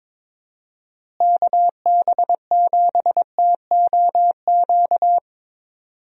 Morse code, heard as a keyed tone, about 22 wpm, 700 Hz. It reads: KB7TOQ
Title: USA Call Signs - 6 Characters - Repeated 2x - 22wpm